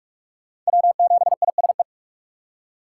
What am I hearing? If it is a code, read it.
Morse code, heard as a keyed tone, about 40 wpm, 700 Hz. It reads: W7ISE